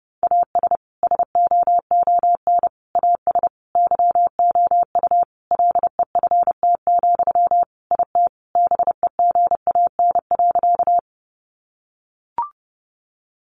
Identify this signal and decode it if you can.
Morse code, heard as a keyed tone, about 30 wpm, 700 Hz. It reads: AS SOON AS YOU LEFT, IT BEGAN.
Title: Sentences from Top 400 Words 30wpm